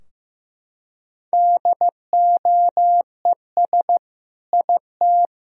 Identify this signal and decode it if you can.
Morse code, heard as a keyed tone, about 15 wpm, 700 Hz. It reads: DOES IT
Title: Sentences from Top 1000 Words - Review Part 2 - 15wpm